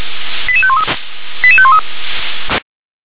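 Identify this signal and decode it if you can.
Five-tone selective call (ZVEI), heard as quick runs of five tones, tones of about 70 ms. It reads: 88513 88513
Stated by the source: ZVEI five-tone call received by SDR.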